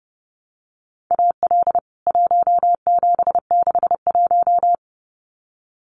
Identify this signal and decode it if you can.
Morse code, heard as a keyed tone, about 30 wpm, 700 Hz. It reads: AL 1761
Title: CWT Exchange 30wpm